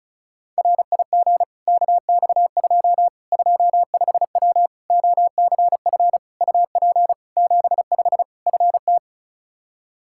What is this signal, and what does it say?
Morse code, heard as a keyed tone, about 36 wpm, 700 Hz. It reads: RIG KX2 25W OCF UP 75 FT
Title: K1USN SST QSO Practice 35wpm